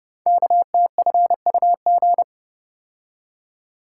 Morse code, heard as a keyed tone, about 30 wpm, 700 Hz. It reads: KTFUG
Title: Code Groups - 5 Letters - 30wpm